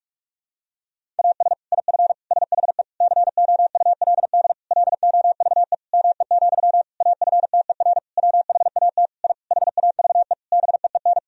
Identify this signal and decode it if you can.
Morse code, heard as a keyed tone, about 46 wpm, 700 Hz. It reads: AS IF SHE COULD LOVE ME, AFTER WHAT I HAVE BEEN
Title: Sentences from Top 1000 Words - Review Part 1 - 45wpm